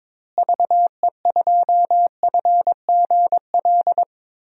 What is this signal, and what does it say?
Morse code, heard as a keyed tone, about 21 wpm, 700 Hz. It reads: VE2FGL
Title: DXCC-100 Call Signs - 6 Characters - Repeated 3x - 22wpm